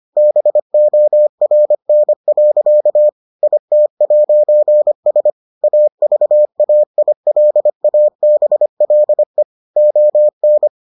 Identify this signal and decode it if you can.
Morse code, heard as a keyed tone, about 25 wpm, 600 Hz. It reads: BORN. IT'S AVAILABLE ON